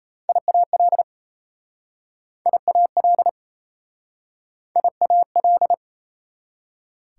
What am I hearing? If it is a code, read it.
Morse code, heard as a keyed tone, about 38 wpm, 690 Hz. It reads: IAL IAL IAL